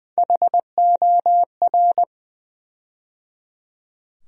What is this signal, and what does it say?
Morse code, heard as a keyed tone, about 20 wpm, 700 Hz. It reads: HOR
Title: USA Call Sign Postfixes - 3 Letters 20wpm